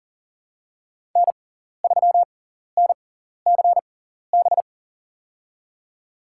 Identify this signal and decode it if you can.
Morse code, heard as a keyed tone, about 40 wpm, spaced as for 14 wpm, 700 Hz. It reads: N3NCB
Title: USA Call Signs - 5 Characters - 6x Character Spacing - 40wpm